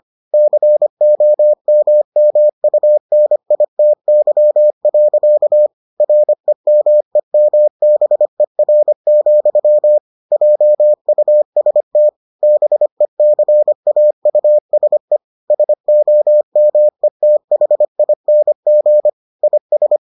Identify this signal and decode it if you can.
Morse code, heard as a keyed tone, about 25 wpm, 600 Hz. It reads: COMMUNITY. REMEMBER, JUST BECAUSE SOMETHING IS